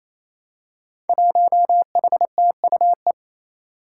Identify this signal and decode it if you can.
Morse code, heard as a keyed tone, about 28 wpm, 700 Hz. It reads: 1HTUE
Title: Five Challenging Letter-Number Combinations 28wpm